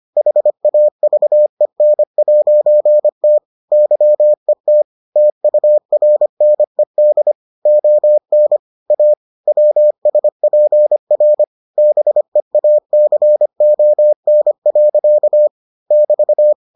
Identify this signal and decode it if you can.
Morse code, heard as a keyed tone, about 25 wpm, 600 Hz. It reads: HAVEN'T YET TURNED ON A WSPR BEACON. <BT>